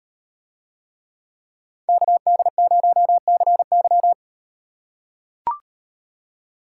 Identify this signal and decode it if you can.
Morse code, heard as a keyed tone, about 38 wpm, 700 Hz. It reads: KD0CY